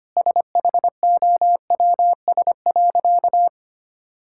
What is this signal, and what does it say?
Morse code, heard as a keyed tone, about 25 wpm, 700 Hz. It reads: SHOWS.
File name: practice / 25wpm / Difficult Sentences with Hints - 25wpm